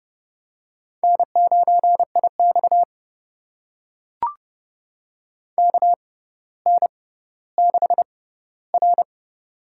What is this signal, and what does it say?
Morse code, heard as a keyed tone, about 30 wpm, spaced as for 15 wpm, 700 Hz. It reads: N9IX KN6R
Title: USA Call Signs - 4 Characters - 6x Character Spacing - 30wpm